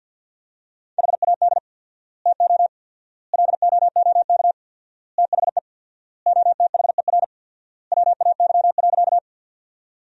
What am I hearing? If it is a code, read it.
Morse code, heard as a keyed tone, about 52 wpm, 700 Hz. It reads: HAD TO LOOK THE OTHER WAY.
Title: 4x Word-Spacing - Sentences from Top 100 Words - 50wpm